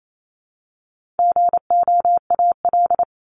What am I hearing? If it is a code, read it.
Morse code, heard as a keyed tone, about 28 wpm, 700 Hz. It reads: GOAL
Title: Top 751-1000 Words 28wpm